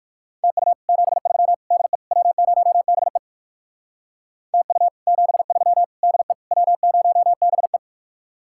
Morse code, heard as a keyed tone, about 56 wpm, 700 Hz. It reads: TU 73 DE W0BE TU 73 DE W0BE
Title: Speed-Racing Common QSO Phrases Encore - 45wpm